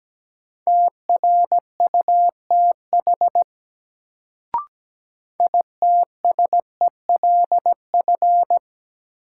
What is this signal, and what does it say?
Morse code, heard as a keyed tone, about 17 wpm, 700 Hz. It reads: TRUTH ITSELF
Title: Sentences from Top 800 Words - New Word Drill - 17wpm